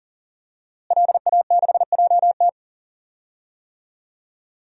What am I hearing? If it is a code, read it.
Morse code, heard as a keyed tone, about 40 wpm, 700 Hz. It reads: LA6JT
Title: DXCC-100 Call Signs - 5 Characters - Repeated 3x - 40wpm